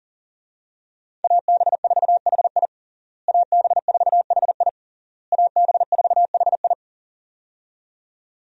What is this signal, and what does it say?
Morse code, heard as a keyed tone, about 40 wpm, 700 Hz. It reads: AB4HI AB4HI AB4HI
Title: USA Call Signs - 5 Characters - Repeated 3x - 40wpm